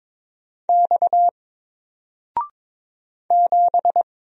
Morse code, heard as a keyed tone, about 22 wpm, 700 Hz. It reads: X 7
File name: Single-Letter-Number-22wpm